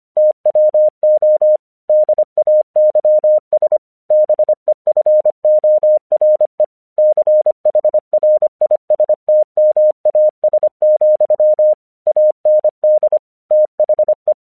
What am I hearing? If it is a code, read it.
Morse code, heard as a keyed tone, about 25 wpm, 610 Hz. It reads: TWO DAYS BEFORE CHRISTMAS, AND THE
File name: 1 English 25wpm / AQC04